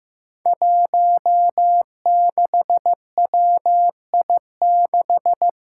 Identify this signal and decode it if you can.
Morse code, heard as a keyed tone, about 15 wpm, 700 Hz. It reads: 16WI6
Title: Five Challenging Letter-Number Combinations 15wpm